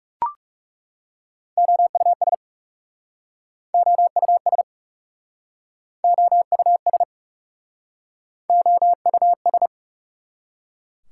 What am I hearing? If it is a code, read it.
Morse code, heard as a keyed tone, about 43 wpm, 700 Hz. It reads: OUS OUS OUS OUS